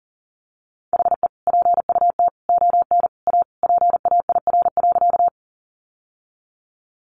Morse code, heard as a keyed tone, about 37 wpm, 700 Hz. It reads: HE PUT ON A PAIR.
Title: Sentences from Top 1200 Words 40wpm